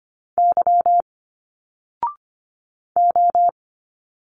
Morse code, heard as a keyed tone, about 25 wpm, 700 Hz. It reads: Y O
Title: Single Letters 25wpm